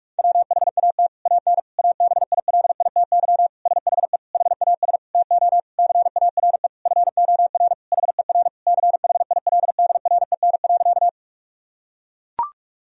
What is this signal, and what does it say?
Morse code, heard as a keyed tone, about 41 wpm, 700 Hz. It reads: WHAT AN ABILITY SHE HAS TO CARE FOR HER CHILDREN.
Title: Sentences from Top 1000 Words - Review Part 3 - 45wpm